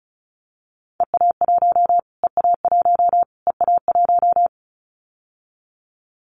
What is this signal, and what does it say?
Morse code, heard as a keyed tone, about 34 wpm, 700 Hz. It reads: EA1 EA1 EA1
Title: Callsign Prefixes for Top 25 DXCC - Repeated 3x - 35wpm